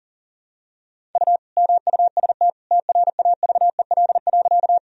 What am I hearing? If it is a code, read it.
Morse code, heard as a keyed tone, about 40 wpm, 700 Hz. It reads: U MUST TRAVEL.
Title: Sentences from Top 400 Words and Abbreviations 40wpm